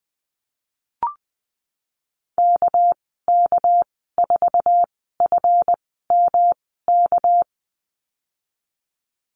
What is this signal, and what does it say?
Morse code, heard as a keyed tone, about 20 wpm, spaced as for 15 wpm, 700 Hz. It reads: KK4FMK
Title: 200 Lesson 32g - Letter k - Call Signs 20wpm